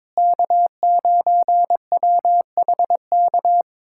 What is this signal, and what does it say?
Morse code, heard as a keyed tone, about 22 wpm, 700 Hz. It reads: K9WHK
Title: USA Callsign DE Callsign - 5 Characters - 22wpm